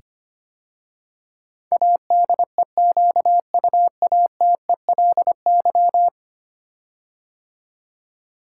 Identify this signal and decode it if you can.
Morse code, heard as a keyed tone, about 25 wpm, 700 Hz. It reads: ADEQUATELY
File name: 10 Letter Words - First 3 Letters Given - 25wpm